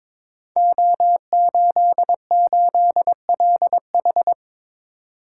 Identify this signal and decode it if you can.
Morse code, heard as a keyed tone, about 23 wpm, 700 Hz. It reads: O88LH
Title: Code Groups - 5 Letter:Numbers - 22wpm